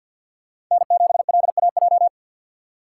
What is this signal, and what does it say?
Morse code, heard as a keyed tone, about 52 wpm, 700 Hz. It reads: N7LAJ
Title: USA Call Signs - 5 Characters - 6x Character Spacing - 50wpm